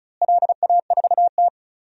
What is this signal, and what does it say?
Morse code, heard as a keyed tone, about 36 wpm, 700 Hz. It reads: LA4T
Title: DXCC-100 Call Signs - 4 Characters - Repeated 3x - 35wpm